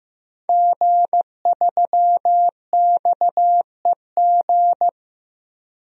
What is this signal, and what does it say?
Morse code, heard as a keyed tone, about 15 wpm, 700 Hz. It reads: G3XEG
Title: DXCC-100 Call Signs - 5 Characters - Repeated 3x - 15wpm